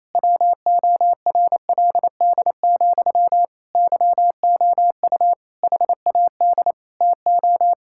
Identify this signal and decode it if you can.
Morse code, heard as a keyed tone, about 27 wpm, 700 Hz. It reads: WORLD, YOU HAD TO
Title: Sentences from Top 1000 Words - Review Part 6 - 28wpm